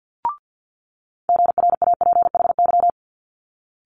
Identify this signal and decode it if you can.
Morse code, heard as a keyed tone, about 50 wpm, 700 Hz. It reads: GRAPHY
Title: Multi-letter Phonemes Quick Fire 50wpm